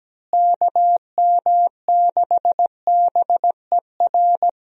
Morse code, heard as a keyed tone, about 17 wpm, 700 Hz. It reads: KM6BER
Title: USA Call Signs - 6 Characters - Repeated 3x - 17wpm